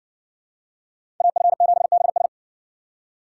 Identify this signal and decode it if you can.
Morse code, heard as a keyed tone, about 60 wpm, 700 Hz. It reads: AF7BS